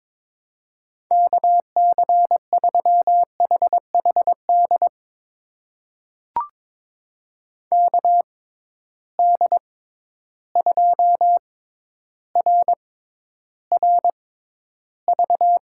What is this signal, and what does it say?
Morse code, heard as a keyed tone, about 21 wpm, spaced as for 11 wpm, 700 Hz. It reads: KC3HHD KD2RRV
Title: USA Call Signs - 6 Characters - 6x Character Spacing - 22wpm